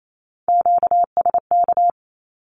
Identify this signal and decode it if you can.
Morse code, heard as a keyed tone, about 28 wpm, 700 Hz. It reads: QSK
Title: Q-Codes Repeated 3x 28wpm